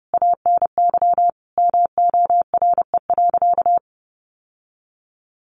Morse code, heard as a keyed tone, about 30 wpm, 700 Hz. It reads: ANY MORE.
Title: Sentences from Top 1300 Words 30wpm